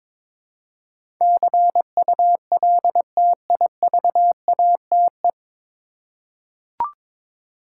Morse code, heard as a keyed tone, about 22 wpm, 700 Hz. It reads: CULTIVATE
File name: 9 Letter Words - First 3 Letters Given - 22wpm